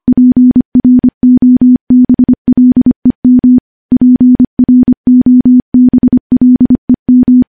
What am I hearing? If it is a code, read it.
Morse code, heard as a keyed tone, about 25 wpm, 250 Hz. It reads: PROBLEM PROBLEM
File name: morse / morse-training-25WPM